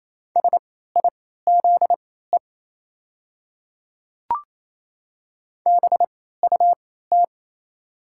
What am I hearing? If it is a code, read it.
Morse code, heard as a keyed tone, about 28 wpm, spaced as for 16 wpm, 700 Hz. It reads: SIZE BUT
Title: 3x Character Spacing - Top 500 Words - 28wpm